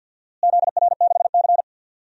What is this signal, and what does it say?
Morse code, heard as a keyed tone, about 48 wpm, 700 Hz. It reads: ZR6C
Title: DXCC-100 Call Signs - 4 Characters - Repeated 3x - 50wpm